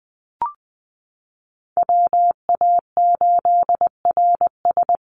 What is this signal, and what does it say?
Morse code, heard as a keyed tone, about 20 wpm, 700 Hz. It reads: WA8RS
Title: USA Call Signs Repeated 3x 20wpm